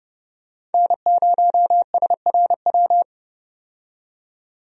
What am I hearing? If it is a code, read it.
Morse code, heard as a keyed tone, about 32 wpm, 700 Hz. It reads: N0SRW
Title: USA Call Signs - 5 Characters - 30wpm